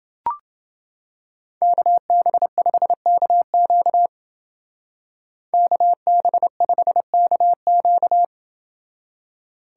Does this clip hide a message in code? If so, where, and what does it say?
Yes: Morse code from 1.6 to 8.2 s, heard as a keyed tone, about 28 wpm, 700 Hz. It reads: KB5KQ KB5KQ